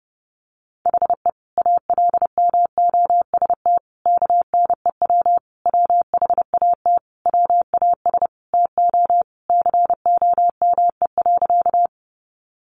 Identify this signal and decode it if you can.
Morse code, heard as a keyed tone, about 30 wpm, 700 Hz. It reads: HE ALMOST KNEW WHAT WAS TO COME.
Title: Sentences from Top 700 Words 30wpm